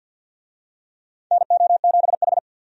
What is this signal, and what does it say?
Morse code, heard as a keyed tone, about 49 wpm, 700 Hz. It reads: NO7H
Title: USA Call Signs - 4 Characters - 50wpm